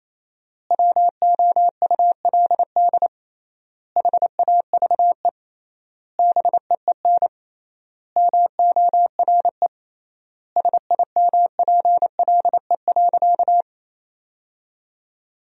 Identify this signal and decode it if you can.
Morse code, heard as a keyed tone, about 28 wpm, 700 Hz. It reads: WOULD HAVE BEEN MORE SIMPLE.
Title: 3x Word-Spacing - Sentences from Top 400 Words - 28wpm